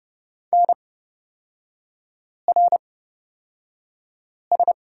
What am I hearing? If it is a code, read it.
Morse code, heard as a keyed tone, about 30 wpm, spaced as for 5 wpm, 700 Hz. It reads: NRS
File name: Lesson 36e - BT - Three Characters 30wpm